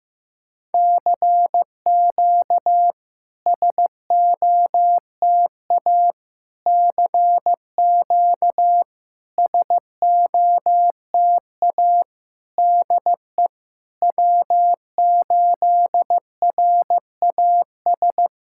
Morse code, heard as a keyed tone, about 15 wpm, 700 Hz. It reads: CQ SOTA CQ SOTA DE W8RAS
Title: SOTA QSO Practice 15wpm